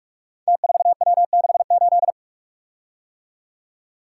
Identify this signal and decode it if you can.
Morse code, heard as a keyed tone, about 44 wpm, 700 Hz. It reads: T4W68
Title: Code Groups - 5 Letter:Numbers - 45wpm